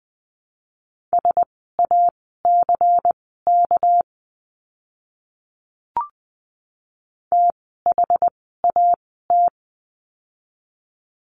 Morse code, heard as a keyed tone, about 20 wpm, spaced as for 14 wpm, 700 Hz. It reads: SACK THAT
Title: 199 Lesson 32f - Letter k - Words 20wpm